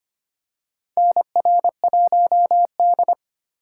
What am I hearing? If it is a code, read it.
Morse code, heard as a keyed tone, about 25 wpm, 700 Hz. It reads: NR1D
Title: USA Call Signs - 4 Characters - 4x Character Spacing - 25wpm